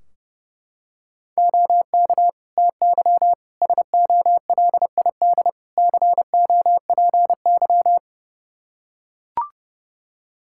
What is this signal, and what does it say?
Morse code, heard as a keyed tone, about 30 wpm, 700 Hz. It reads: OK TY SOLID COPY
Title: K1USN SST QSO Practice 30wpm